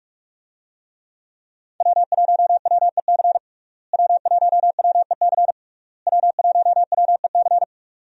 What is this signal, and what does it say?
Morse code, heard as a keyed tone, about 46 wpm, 700 Hz. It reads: W1WEC W1WEC W1WEC
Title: USA Call Signs - 5 Characters - Repeated 3x - 45wpm